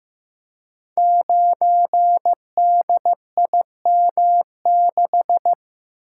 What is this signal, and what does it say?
Morse code, heard as a keyed tone, about 15 wpm, 700 Hz. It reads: 9DIM6
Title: Code Groups - 5 Letter:Numbers - 15wpm